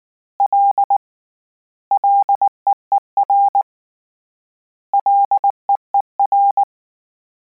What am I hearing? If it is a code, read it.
Morse code, heard as a keyed tone, about 19 wpm, 800 Hz. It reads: L LEER LEER
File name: Combinations / Combinations-mp3s / Combinations@18